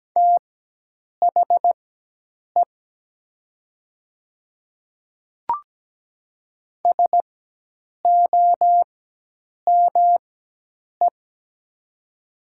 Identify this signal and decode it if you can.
Morse code, heard as a keyed tone, about 17 wpm, spaced as for 8 wpm, 700 Hz. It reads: THE SOME
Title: 4x Character Spacing - Top 200 Words - 17wpm